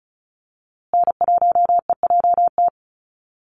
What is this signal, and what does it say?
Morse code, heard as a keyed tone, about 34 wpm, 700 Hz. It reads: N1EJT
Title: USA Call Signs Repeated 2x 35wpm